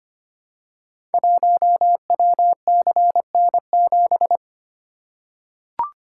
Morse code, heard as a keyed tone, about 25 wpm, 700 Hz. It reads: 1WCN7
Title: Code Groups - 5 Letter:Numbers - 25wpm